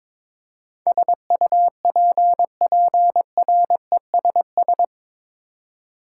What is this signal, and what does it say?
Morse code, heard as a keyed tone, about 21 wpm, 700 Hz. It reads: SUPPRESS
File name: practice / 22wpm / Eight Letter Words - 22wpm